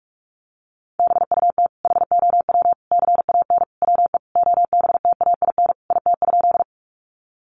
Morse code, heard as a keyed tone, about 44 wpm, 700 Hz. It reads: BUT HOW CAN WE OBTAIN IT?
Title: Sentences from Top 1300 Words 45wpm